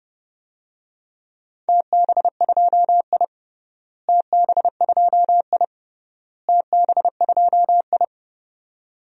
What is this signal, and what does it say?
Morse code, heard as a keyed tone, about 32 wpm, 700 Hz. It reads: TB2I TB2I TB2I
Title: DXCC-100 Call Signs - 4 Characters - Repeated 3x - 30wpm